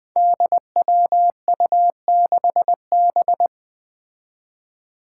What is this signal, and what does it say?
Morse code, heard as a keyed tone, about 20 wpm, 700 Hz. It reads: DWU6B
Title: Five Challenging Letter-Number Combinations 20wpm